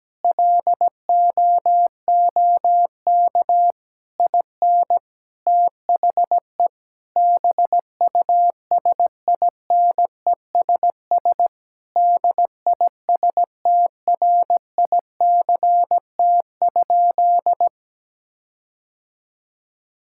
Morse code, heard as a keyed tone, about 17 wpm, 700 Hz. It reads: LOOK IN THE BUSINESS DISTRICT?
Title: Sentences from Top 1600 Words 17wpm